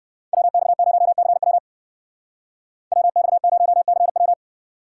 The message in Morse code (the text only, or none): WX0CW WX0CW